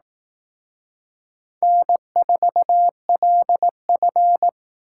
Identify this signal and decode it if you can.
Morse code, heard as a keyed tone, about 19 wpm, 700 Hz. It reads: N4LF
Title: Speed Racing - USA 1x2 Call Signs 15wpm